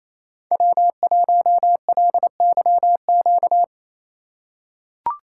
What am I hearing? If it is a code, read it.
Morse code, heard as a keyed tone, about 28 wpm, 700 Hz. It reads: W1LYQ
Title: USA Call Signs - 4x Character Spacing - 28wpm